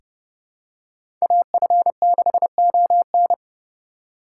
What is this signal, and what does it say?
Morse code, heard as a keyed tone, about 30 wpm, 700 Hz. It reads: AF6ON
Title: USA Call Signs - 5 Characters - 4x Character Spacing - 30wpm